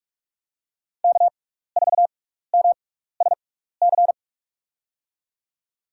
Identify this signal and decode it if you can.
Morse code, heard as a keyed tone, about 44 wpm, spaced as for 15 wpm, 700 Hz. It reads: K4MSC
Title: USA Call Signs - 5 Characters - 6x Character Spacing - 45wpm